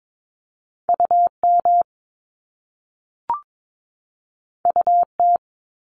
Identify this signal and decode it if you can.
Morse code, heard as a keyed tone, about 22 wpm, 700 Hz. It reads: UM UT
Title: Top Two Letter Combinations in Words 22wpm